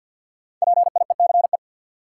Morse code, heard as a keyed tone, about 53 wpm, 700 Hz. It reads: PIECE
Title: Sentences from Top 400 Words - New Word Drill - 50wpm